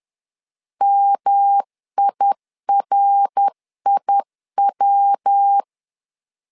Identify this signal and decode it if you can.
Morse code, heard as a keyed tone, about 10 wpm, 800 Hz. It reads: MIRIW
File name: LEZIONI / 3 MWRO / 3 MWR 10wpm 1952